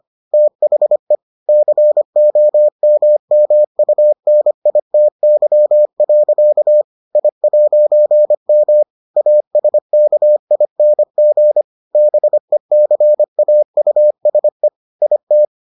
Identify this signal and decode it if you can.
Morse code, heard as a keyed tone, about 25 wpm, 600 Hz. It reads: THE COMMUNITY. I'M ASKING BECAUSE IT